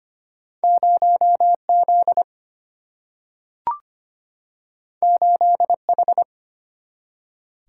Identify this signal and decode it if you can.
Morse code, heard as a keyed tone, about 25 wpm, 700 Hz. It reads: 0 Z 8 H
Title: Single-Letter Single-Number 25wpm